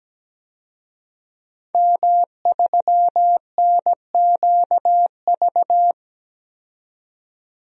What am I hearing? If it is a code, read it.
Morse code, heard as a keyed tone, about 17 wpm, 700 Hz. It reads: M3NQV
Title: DXCC-100 Call Signs - 5 Characters - 17wpm